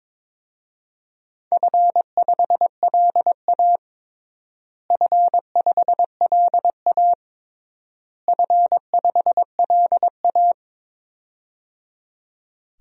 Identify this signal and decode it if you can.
Morse code, heard as a keyed tone, about 22 wpm, 700 Hz. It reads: F5LA F5LA F5LA
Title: DXCC-100 Call Signs - 4 Characters - Repeated 3x - 22wpm